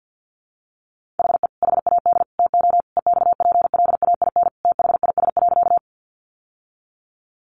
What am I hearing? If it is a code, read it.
Morse code, heard as a keyed tone, about 49 wpm, 700 Hz. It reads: HE HAD TO EXPLAIN THIS.
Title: Sentences from Top 1100 Words 50wpm